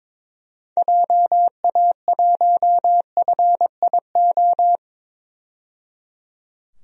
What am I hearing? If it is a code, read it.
Morse code, heard as a keyed tone, about 22 wpm, 700 Hz. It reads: JA1FIO
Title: DXCC-100 Call Signs - 6 Characters - 22wpm